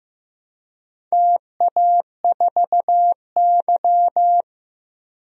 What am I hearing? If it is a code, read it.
Morse code, heard as a keyed tone, about 15 wpm, 700 Hz. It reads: TA4Y
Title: DXCC-100 Call Signs - 4 Characters - Repeated 3x - 15wpm